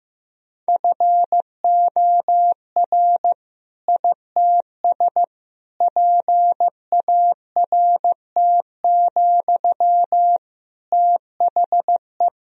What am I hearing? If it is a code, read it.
Morse code, heard as a keyed tone, about 15 wpm, 700 Hz. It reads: FOR ITS PART, THE
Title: Sentences from Top 1000 Words - Review Part 3 - 15wpm